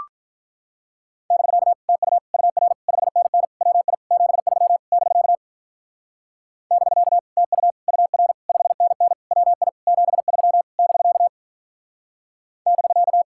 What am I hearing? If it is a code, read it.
Morse code, heard as a keyed tone, about 43 wpm, 700 Hz. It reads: <BK> TU UR 5NN WI 73 <BK> <BK> TU UR 5NN WI 73 <BK> <BK>
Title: Speed-Racing Common QSO Phrases - 35wpm